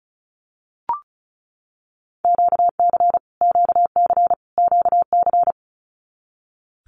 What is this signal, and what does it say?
Morse code, heard as a keyed tone, about 34 wpm, 700 Hz. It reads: QC QC QC
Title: Canadian Provinces and Territories - Repeated 3x - 35wpm